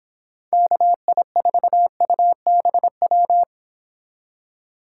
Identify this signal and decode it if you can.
Morse code, heard as a keyed tone, about 25 wpm, 700 Hz. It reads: KI4UBW
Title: Speed Racing - USA 2x3 Call Signs 22wpm